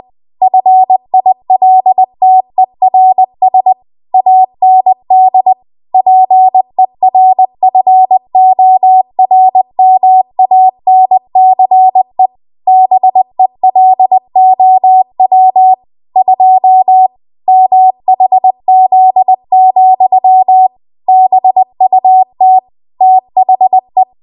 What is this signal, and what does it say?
Morse code, heard as a keyed tone, about 20 wpm, 750 Hz. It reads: FILTERS AND PERFORMANCE BELOW 2 MHZ, BUT THE
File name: Archive / 20 WPM / 20WPM